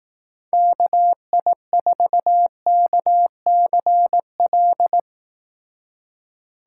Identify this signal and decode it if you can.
Morse code, heard as a keyed tone, about 18 wpm, 700 Hz. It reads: KI4KCL
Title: Speed Racing - USA Call Signs 15wpm